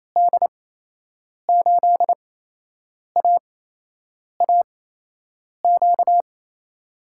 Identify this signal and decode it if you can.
Morse code, heard as a keyed tone, about 28 wpm, spaced as for 8 wpm, 700 Hz. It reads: D8AAQ
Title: USA Call Signs - 6 Characters - 8x Character Spacing - 28wpm